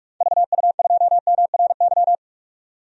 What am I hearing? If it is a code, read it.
Morse code, heard as a keyed tone, about 44 wpm, 700 Hz. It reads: VU2MRY
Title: DXCC-100 Call Signs - 6 Characters - 45wpm